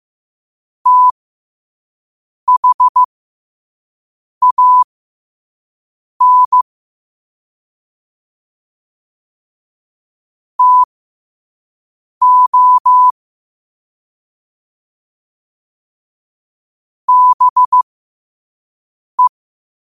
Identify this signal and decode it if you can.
Morse code, heard as a keyed tone, about 15 wpm, spaced as for 5 wpm, 1000 Hz. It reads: THAN TO BE
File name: qotdfiles / 5wpm